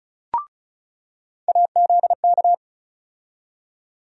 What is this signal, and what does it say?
Morse code, heard as a keyed tone, about 36 wpm, 700 Hz. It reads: AZK
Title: USA Call Sign Postfixes - 3 Letters 35wpm